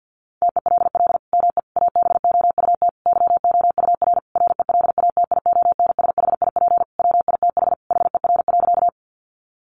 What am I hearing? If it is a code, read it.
Morse code, heard as a keyed tone, about 50 wpm, 700 Hz. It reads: TELL ME ABOUT YOUR RELATIONSHIP WITH HER.
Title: Sentences from Top 1100 Words 50wpm